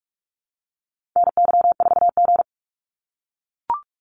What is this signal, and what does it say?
Morse code, heard as a keyed tone, about 44 wpm, 700 Hz. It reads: NY4G
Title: NA SOTA Activator Callsigns 45wpm